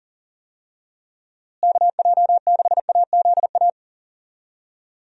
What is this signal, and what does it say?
Morse code, heard as a keyed tone, about 40 wpm, 700 Hz. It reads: KJ6AZA
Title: USA Call Signs - 6 Characters - Repeated 3x - 40wpm